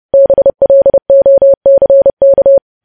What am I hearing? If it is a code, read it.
Morse code, heard as a keyed tone, about 30 wpm, 550 Hz. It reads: BLOCK
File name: podcast / News headlines-1x-30x30